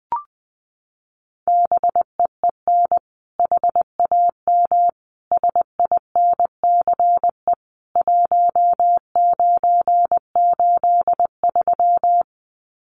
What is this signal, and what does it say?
Morse code, heard as a keyed tone, about 20 wpm, 700 Hz. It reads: BEEN HAM SINCE 1983
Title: Common QSO Phrases 20wpm